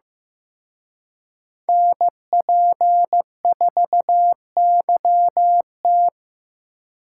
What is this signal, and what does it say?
Morse code, heard as a keyed tone, about 17 wpm, 700 Hz. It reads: NP4YT
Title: Speed Racing - USA 2x2 Call Signs 15wpm